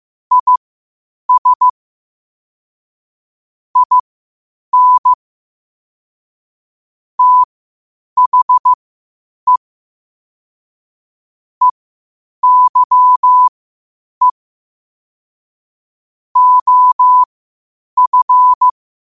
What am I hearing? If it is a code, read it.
Morse code, heard as a keyed tone, about 15 wpm, spaced as for 8 wpm, 1000 Hz. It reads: IS IN THE EYE OF
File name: qotdfiles / 8wpm